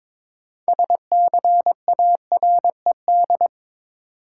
Morse code, heard as a keyed tone, about 22 wpm, 700 Hz. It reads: SCARED
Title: Sentences from Top 700 Words - New Word Drill - 22wpm